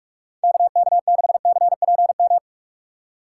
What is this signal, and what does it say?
Morse code, heard as a keyed tone, about 46 wpm, 700 Hz. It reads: KK6CPM
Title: USA Call Signs - 6 Characters - 6x Character Spacing - 45wpm